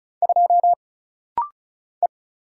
Morse code, heard as a keyed tone, about 34 wpm, 700 Hz. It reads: 2 E
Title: Single Letter-Number Alt - Rapid-Fire 35wpm